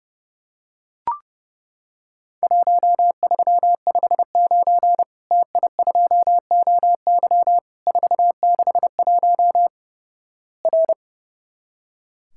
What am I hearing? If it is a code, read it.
Morse code, heard as a keyed tone, about 32 wpm, 700 Hz. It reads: 1359 TI2OY 461 R
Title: Worked All Europe DX Contest - QTC Exchange - 30wpm